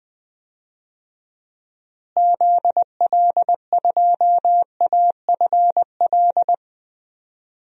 Morse code, heard as a keyed tone, about 21 wpm, 700 Hz. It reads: ZL2AFL
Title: DXCC-100 Call Signs - 6 Characters - Repeated 2x - 20wpm